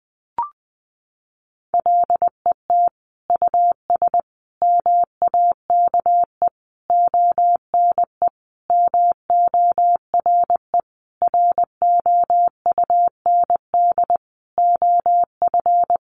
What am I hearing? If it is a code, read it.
Morse code, heard as a keyed tone, about 20 wpm, 700 Hz. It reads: LET US MAKE ONE MORE ROUND OF
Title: Sentences from Top 700 Words 20wpm